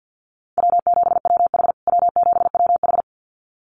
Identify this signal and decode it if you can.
Morse code, heard as a keyed tone, about 48 wpm, 700 Hz. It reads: W7WH W7WH
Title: USA Call Signs Repeated 2x 50wpm